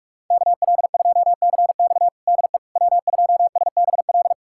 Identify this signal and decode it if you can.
Morse code, heard as a keyed tone, about 46 wpm, 700 Hz. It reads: KL2CX DE W2SBL
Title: USA Callsign DE Callsign - 5 Characters - 45wpm